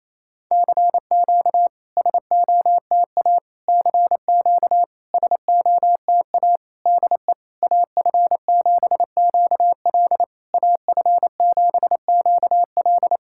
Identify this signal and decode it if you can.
Morse code, heard as a keyed tone, about 28 wpm, 690 Hz. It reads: CQ SOTA CQ SOTA DE AF7QL AF7QL
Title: SOTA QSO Practice 28wpm